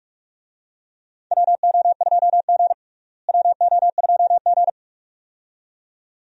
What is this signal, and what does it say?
Morse code, heard as a keyed tone, about 45 wpm, 700 Hz. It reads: WO2G WO2G
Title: USA Call Signs - 4 Characters - Repeated 2x - 45wpm